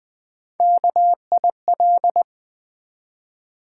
Morse code, heard as a keyed tone, about 20 wpm, 700 Hz. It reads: KIL
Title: Build A Word - 3 Letters At A Time - 9-Letter Words - 20wpm